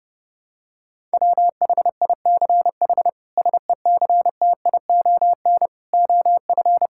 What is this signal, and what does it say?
Morse code, heard as a keyed tone, about 32 wpm, 700 Hz. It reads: WHICH SECTION OF
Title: Sentences from Top 1000 Words - Review Part 5 - 30wpm